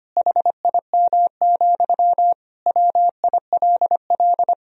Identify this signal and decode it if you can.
Morse code, heard as a keyed tone, about 25 wpm, 700 Hz. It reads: HIM, WILL
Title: Sentences from Top 1000 Words - Review Part 1 - 25wpm